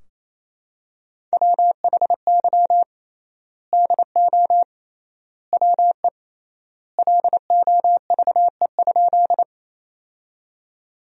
Morse code, heard as a keyed tone, about 28 wpm, 700 Hz. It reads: WHY DO WE LOVE?
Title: 3x Word-Spacing - Sentences from Top 400 Words - 28wpm